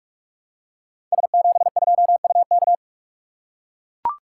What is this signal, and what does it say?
Morse code, heard as a keyed tone, about 44 wpm, 700 Hz. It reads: S72UK
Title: Code Groups - 5 Letter:Numbers - 45wpm